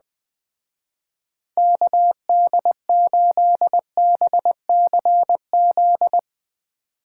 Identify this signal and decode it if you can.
Morse code, heard as a keyed tone, about 20 wpm, 700 Hz. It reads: KD8BCZ